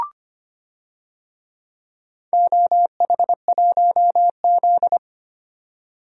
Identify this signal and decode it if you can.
Morse code, heard as a keyed tone, about 25 wpm, 700 Hz. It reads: OH1Z
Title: DXCC-100 Call Signs - 4 Characters - 25wpm